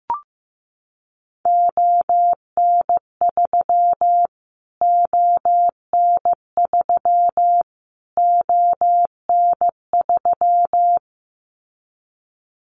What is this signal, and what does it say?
Morse code, heard as a keyed tone, about 15 wpm, 700 Hz. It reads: ON3 ON3 ON3
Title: Callsign Prefixes for Top 25 DXCC - Repeated 3x - 15wpm